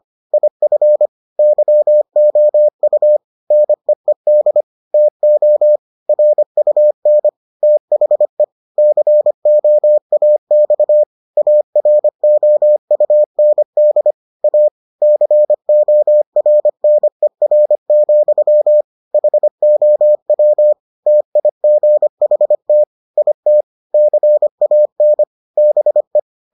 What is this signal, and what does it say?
Morse code, heard as a keyed tone, about 25 wpm, 600 Hz. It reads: IF YOU NEED TO RUN THE COAX AROUND A CORNER, HOW TIGHT IT CAN BE